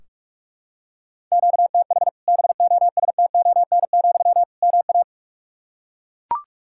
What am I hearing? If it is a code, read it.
Morse code, heard as a keyed tone, about 41 wpm, 700 Hz. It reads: QTH BOSTON, MA
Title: Practice QSO - Exchange 1 - Beginning - 45wpm